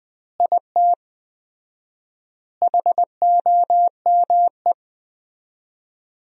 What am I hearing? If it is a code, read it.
Morse code, heard as a keyed tone, about 20 wpm, 700 Hz. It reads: IT HOME
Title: 4x Word-Spacing - Sentences from Top 500 Words - 20wpm